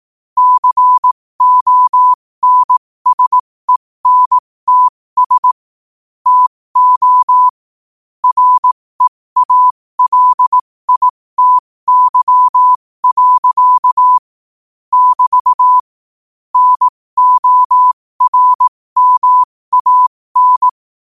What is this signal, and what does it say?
Morse code, heard as a keyed tone, about 18 wpm, spaced as for 15 wpm, 1000 Hz. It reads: CONSENTS TO REALITY. <BT> NORMAN